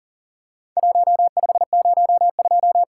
Morse code, heard as a keyed tone, about 40 wpm, 700 Hz. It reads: 1502
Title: Worked All Europe DX Contest - QTC Exchange - 40wpm